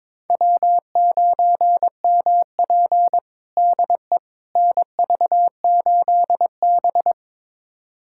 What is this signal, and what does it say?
Morse code, heard as a keyed tone, about 22 wpm, 700 Hz. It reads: W9MP DE NV8B
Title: USA Callsign DE Callsign - 4 Characters - 22wpm